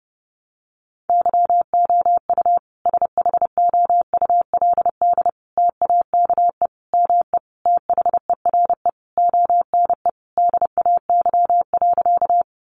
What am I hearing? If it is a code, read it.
Morse code, heard as a keyed tone, about 30 wpm, 700 Hz. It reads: YOU SHOULD TAKE ME THERE ONE DAY.
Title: Sentences from Top 300 Words 30wpm